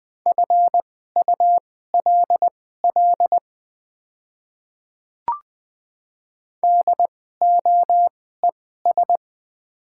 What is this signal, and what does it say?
Morse code, heard as a keyed tone, about 20 wpm, spaced as for 14 wpm, 700 Hz. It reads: FULL DOES
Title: 2x Character Spacing - Top 500 Words - 20wpm